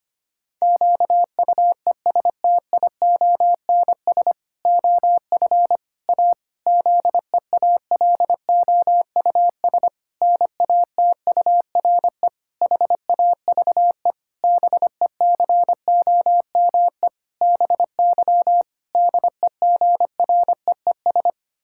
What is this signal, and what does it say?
Morse code, heard as a keyed tone, about 25 wpm, 700 Hz. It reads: QUESTIONS OF A ZEALOUS NATURE HAVE BECOME BY DEGREES
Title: Pangram Challenge v2 - 25wpm